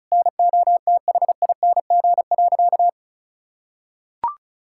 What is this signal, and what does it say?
Morse code, heard as a keyed tone, about 34 wpm, 700 Hz. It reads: NOTHING.
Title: Sentences from Top 1000 Words - Review Part 6 - 35wpm